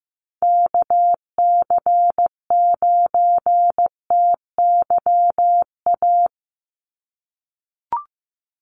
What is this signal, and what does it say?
Morse code, heard as a keyed tone, about 15 wpm, 700 Hz. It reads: KC9TYA
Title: NA SOTA Chaser Callsigns 15wpm